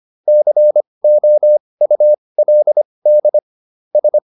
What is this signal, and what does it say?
Morse code, heard as a keyed tone, about 25 wpm, spaced as for 20 wpm, 600 Hz. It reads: COULD S
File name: SS215 20